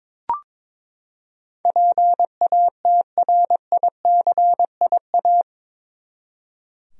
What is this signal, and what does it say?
Morse code, heard as a keyed tone, about 21 wpm, 700 Hz. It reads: PATRICIA
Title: USA Common Names - 22wpm